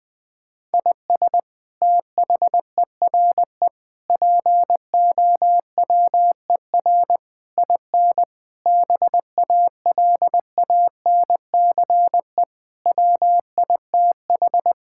Morse code, heard as a keyed tone, about 20 wpm, 700 Hz. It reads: IS THERE POWER IN BALANCE WITH
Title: Sentences from Top 1000 Words - Review Part 6 - 20wpm